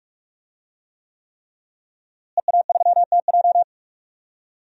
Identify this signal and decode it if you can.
Morse code, heard as a keyed tone, about 46 wpm, 700 Hz. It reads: EA3TJ
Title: DXCC-100 Call Signs - 5 Characters - Repeated 3x - 45wpm